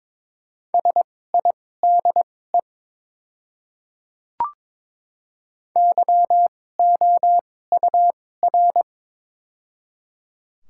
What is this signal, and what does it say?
Morse code, heard as a keyed tone, about 22 wpm, spaced as for 16 wpm, 700 Hz. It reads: SIDE YOUR
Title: 2x Character Spacing - Top 200 Words - 22wpm